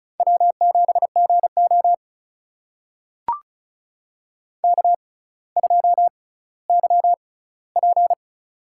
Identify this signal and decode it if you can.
Morse code, heard as a keyed tone, about 36 wpm, spaced as for 18 wpm, 700 Hz. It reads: W7GO K2YP
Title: USA Call Signs - 4 Characters - 6x Character Spacing - 35wpm